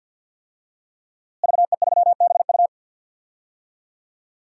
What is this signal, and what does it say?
Morse code, heard as a keyed tone, about 52 wpm, 700 Hz. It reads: VE3BU
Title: DXCC-100 Call Signs - 5 Characters - Repeated 3x - 50wpm